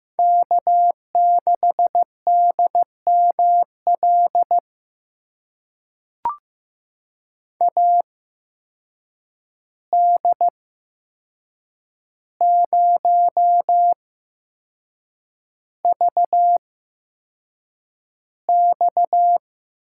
Morse code, heard as a keyed tone, about 15 wpm, spaced as for 6 wpm, 700 Hz. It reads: K6DML AD0VX
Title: USA Call Signs - 5 Characters - 8x Character Spacing - 15wpm